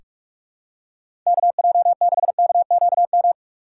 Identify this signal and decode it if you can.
Morse code, heard as a keyed tone, about 46 wpm, 700 Hz. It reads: KJ6KQM
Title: USA Call Signs - 6 Characters - 3x Character Spacing - 45wpm